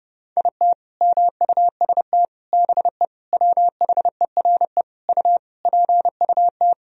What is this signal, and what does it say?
Morse code, heard as a keyed tone, about 32 wpm, 700 Hz. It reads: IT MUST BE WHERE U PUT
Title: Sentences from Top 200 Words and Abbreviations 30wpm